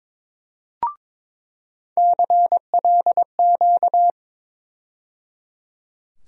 Morse code, heard as a keyed tone, about 21 wpm, 700 Hz. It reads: CLQ
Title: USA Call Sign Postfixes - 3 Letters 22wpm